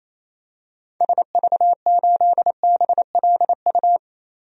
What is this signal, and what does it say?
Morse code, heard as a keyed tone, about 27 wpm, 700 Hz. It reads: SV8BLU
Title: DXCC-100 Call Signs - 6 Characters - Repeated 2x - 28wpm